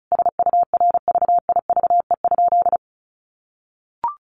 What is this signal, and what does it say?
Morse code, heard as a keyed tone, about 34 wpm, 700 Hz. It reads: SURVIVE?
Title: A Sound Of Thunder 35wpm